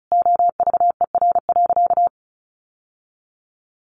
Morse code, heard as a keyed tone, about 34 wpm, 700 Hz. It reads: OVER.
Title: Sentences from Top 1100 Words 35wpm